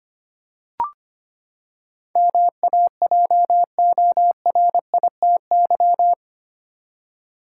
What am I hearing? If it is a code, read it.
Morse code, heard as a keyed tone, about 25 wpm, 700 Hz. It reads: MAJORITY
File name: Eight Letter Words - 25wpm